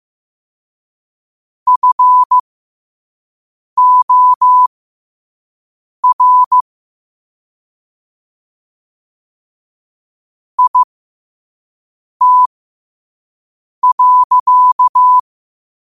Morse code, heard as a keyed tone, about 15 wpm, spaced as for 5 wpm, 1000 Hz. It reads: FOR IT.
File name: qotdfiles / 5wpm